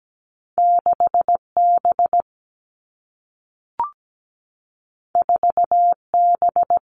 Morse code, heard as a keyed tone, about 17 wpm, 700 Hz. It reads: 6 B 4 B
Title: Two Challenging Letter-Number Combinations 17wpm